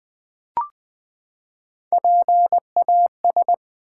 Morse code, heard as a keyed tone, about 20 wpm, 700 Hz. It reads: PAS
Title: Build A Word - 3 Letters At A Time - 9-Letter Words - 20wpm